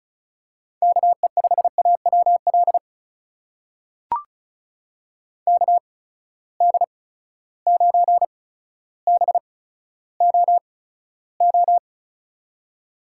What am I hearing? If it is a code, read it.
Morse code, heard as a keyed tone, about 36 wpm, spaced as for 15 wpm, 700 Hz. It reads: KE5AWL KD9BOO
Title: USA Call Signs - 6 Characters - 8x Character Spacing - 35wpm